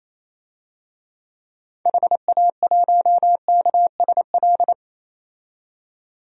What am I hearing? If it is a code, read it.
Morse code, heard as a keyed tone, about 28 wpm, 700 Hz. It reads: HA1KSL
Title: DXCC-100 Call Signs - 6 Characters - 28wpm